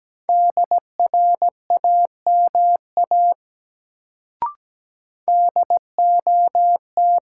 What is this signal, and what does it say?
Morse code, heard as a keyed tone, about 17 wpm, 700 Hz. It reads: DRAMA DOT
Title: Sentences from Top 1500 Words - New Word Drill - 17wpm